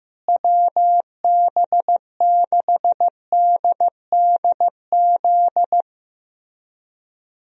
Morse code, heard as a keyed tone, about 16 wpm, 700 Hz. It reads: WB6DDZ
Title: USA Call Signs - 6 Characters - Repeated 3x - 15wpm